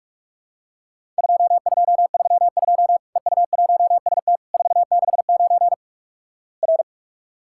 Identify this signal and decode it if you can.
Morse code, heard as a keyed tone, about 46 wpm, 700 Hz. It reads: 2232 EU1ST 469 R
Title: Worked All Europe DX Contest - QTC Exchange - 45wpm